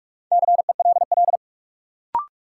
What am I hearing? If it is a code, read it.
Morse code, heard as a keyed tone, about 46 wpm, 700 Hz. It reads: CELL
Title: Sentences from Top 1100 Words - New Word Drill - 45wpm